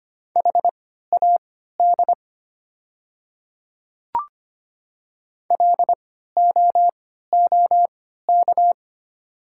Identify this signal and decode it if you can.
Morse code, heard as a keyed tone, about 25 wpm, spaced as for 14 wpm, 700 Hz. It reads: HAD LOOK
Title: 3x Character Spacing - Top 200 Words - 25wpm